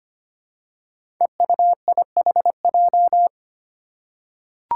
Morse code, heard as a keyed tone, about 25 wpm, 700 Hz. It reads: EUIHJ
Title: Five Challenging Letter-Number Combinations 25wpm